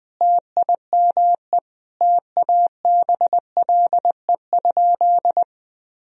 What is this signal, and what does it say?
Morse code, heard as a keyed tone, about 20 wpm, 700 Hz. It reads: TIME TABLE?
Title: Sentences from Top 1000 Words - Review Part 2 - 20wpm